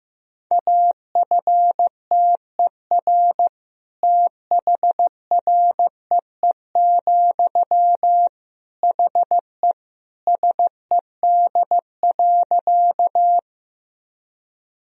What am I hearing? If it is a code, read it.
Morse code, heard as a keyed tone, about 15 wpm, 700 Hz. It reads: AFTER THREE, HE SED.
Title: Sentences from Top 200 Words and Abbreviations 15wpm